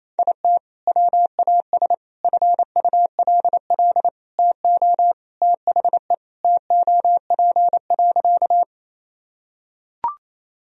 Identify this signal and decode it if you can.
Morse code, heard as a keyed tone, about 28 wpm, 700 Hz. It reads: IT WAS FULL TO THE TOP.
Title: Sentences from Top 1000 Words - Review Part 3 - 28wpm